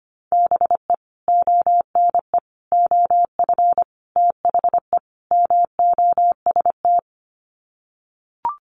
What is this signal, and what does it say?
Morse code, heard as a keyed tone, about 25 wpm, 700 Hz. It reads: BE ONE OF THE MOST
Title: Sets of 5 Words Encore 25wpm